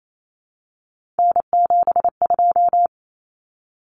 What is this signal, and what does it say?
Morse code, heard as a keyed tone, about 28 wpm, 700 Hz. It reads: N72
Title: Common Cut Numbers - 3-Digit 28wpm